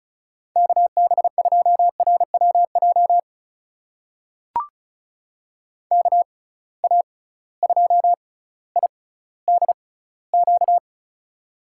Morse code, heard as a keyed tone, about 36 wpm, spaced as for 18 wpm, 700 Hz. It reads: KB2RWJ KA2IDQ
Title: USA Call Signs - 6 Characters - 6x Character Spacing - 35wpm